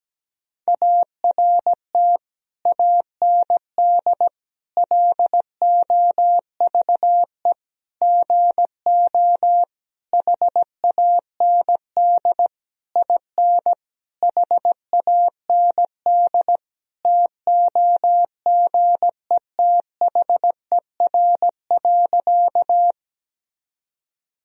Morse code, heard as a keyed tone, about 17 wpm, 700 Hz. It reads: ART AND LOVE GO HAND IN HAND TOGETHER.
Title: Sentences from Top 1000 Words - Review Part 3 - 17wpm